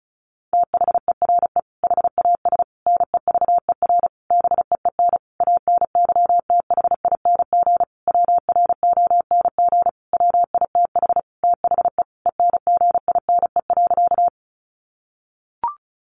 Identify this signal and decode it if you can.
Morse code, heard as a keyed tone, about 34 wpm, 700 Hz. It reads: THERE HAS NEVER BEEN ANYTHING WRONG WITH THE ENGINE.
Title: Sentences from Top 1500 Words 35wpm